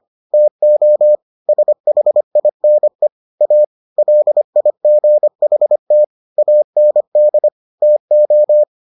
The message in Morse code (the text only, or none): TO SHINE A LIGHT AND TO